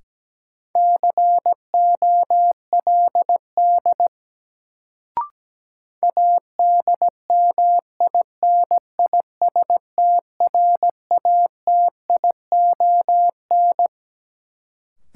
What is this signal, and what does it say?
Morse code, heard as a keyed tone, about 17 wpm, 700 Hz. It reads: COLD ADMINISTRATION
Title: Sets of 5 Words - New Word Drill - 17wpm